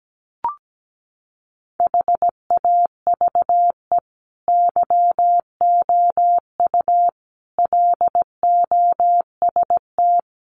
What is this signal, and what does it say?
Morse code, heard as a keyed tone, about 17 wpm, 700 Hz. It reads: HAVE YOU LOST
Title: Sentences from Top 1400 Words 17wpm